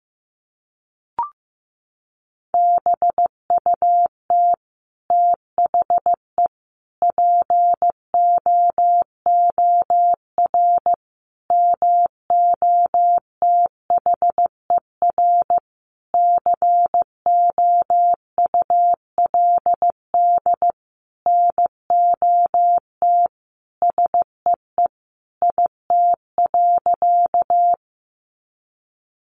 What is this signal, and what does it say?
Morse code, heard as a keyed tone, about 15 wpm, 700 Hz. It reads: BUT THE POOR MOTHER COULD NOT SEE IT.
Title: Sentences from Top 1000 Words 15wpm